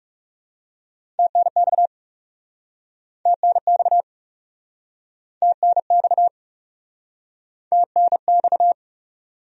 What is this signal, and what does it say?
Morse code, heard as a keyed tone, about 41 wpm, 700 Hz. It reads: TNX TNX TNX TNX